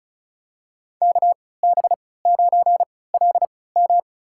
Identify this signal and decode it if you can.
Morse code, heard as a keyed tone, about 35 wpm, spaced as for 20 wpm, 700 Hz. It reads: KB9LM
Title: USA Call Signs - 5 Characters - 3x Character Spacing - 35wpm